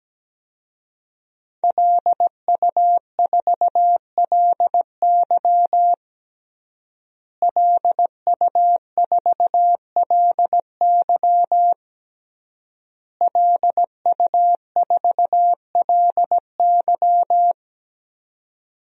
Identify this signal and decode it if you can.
Morse code, heard as a keyed tone, about 17 wpm, 700 Hz. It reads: LU4LY LU4LY LU4LY
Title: DXCC-100 Call Signs - 5 Characters - Repeated 3x - 17wpm